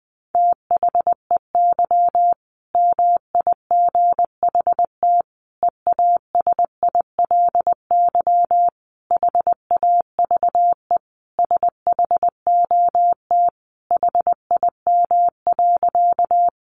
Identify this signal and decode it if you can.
Morse code, heard as a keyed tone, about 19 wpm, 700 Hz. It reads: THEY MIGHT EASILY HAVE SHOT HIM.
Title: Sentences from Top 1100 Words 20wpm